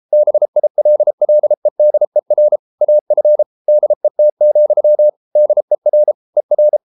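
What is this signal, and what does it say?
Morse code, heard as a keyed tone, about 33 wpm, 600 Hz. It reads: BILLEDER AF DET, DER ER
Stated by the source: Morse code at 33 wpm, QRQ.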